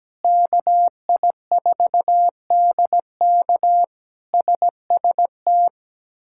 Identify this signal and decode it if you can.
Morse code, heard as a keyed tone, about 17 wpm, 700 Hz. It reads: KI4DK SST
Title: K1USN SST QSO Practice - 17wpm